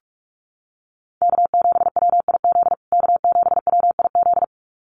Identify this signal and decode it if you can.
Morse code, heard as a keyed tone, about 44 wpm, 700 Hz. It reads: K7WIZ K7WIZ
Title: USA Call Signs Repeated 2x 45wpm